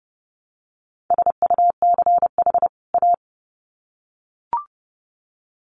Morse code, heard as a keyed tone, about 30 wpm, 700 Hz. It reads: SUCH A
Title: Sets of 2 Words 30wpm